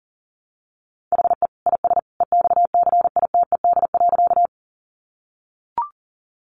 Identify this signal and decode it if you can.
Morse code, heard as a keyed tone, about 40 wpm, 700 Hz. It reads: HE IS EXCITED.
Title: Additional Easy Phrases 40wpm